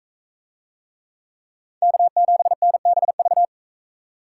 Morse code, heard as a keyed tone, about 41 wpm, 700 Hz. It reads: K7NBV